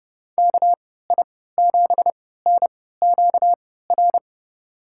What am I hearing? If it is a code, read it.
Morse code, heard as a keyed tone, about 30 wpm, spaced as for 17 wpm, 700 Hz. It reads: KI7NQR
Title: USA Call Signs - 6 Characters - 3x Character Spacing - 30wpm